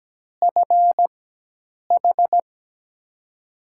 Morse code, heard as a keyed tone, about 17 wpm, spaced as for 8 wpm, 700 Hz. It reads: FH
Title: USA Call Signs - 4x Character Spacing - 17wpm